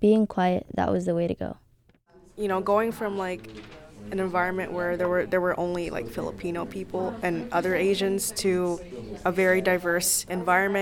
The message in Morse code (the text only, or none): none